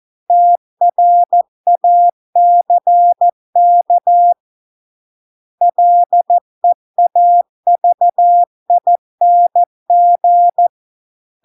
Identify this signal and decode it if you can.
Morse code, heard as a keyed tone, about 14 wpm, 700 Hz. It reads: TRACK LEAVING